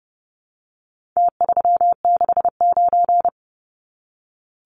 Morse code, heard as a keyed tone, about 30 wpm, 700 Hz. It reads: T369
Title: Numbers - Contest Style - 4-Digit 30wpm